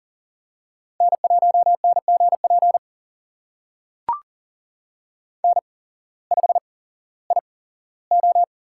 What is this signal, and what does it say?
Morse code, heard as a keyed tone, about 39 wpm, spaced as for 19 wpm, 700 Hz. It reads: N1NGP N5IO
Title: USA Call Signs - 5 Characters - 8x Character Spacing - 40wpm